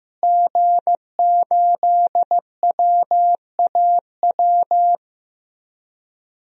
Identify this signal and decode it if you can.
Morse code, heard as a keyed tone, about 15 wpm, 700 Hz. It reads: G8WAW
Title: DXCC-100 Call Signs - 5 Characters - Repeated 3x - 15wpm